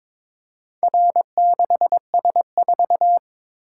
Morse code, heard as a keyed tone, about 22 wpm, 700 Hz. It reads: R6S4
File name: practice / 22wpm / Four Letter-Number Combinations - 22wpm